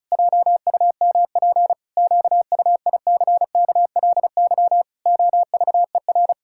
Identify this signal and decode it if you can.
Morse code, heard as a keyed tone, about 36 wpm, 700 Hz. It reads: JUMP QUICKLY OVER
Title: Pangram Challenge v2 - 35wpm